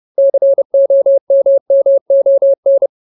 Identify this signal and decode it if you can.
Morse code, heard as a keyed tone, about 30 wpm, 550 Hz. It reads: COMMON
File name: Ultimate Word List / 30WPM Ultimate